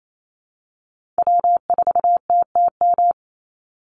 Morse code, heard as a keyed tone, about 28 wpm, 700 Hz. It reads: W4TTM
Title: Call Signs Repeated 3x - 28wpm